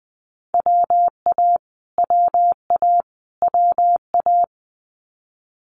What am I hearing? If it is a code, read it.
Morse code, heard as a keyed tone, about 20 wpm, 700 Hz. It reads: WA WA WA
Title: US States Repeated 3x 20wpm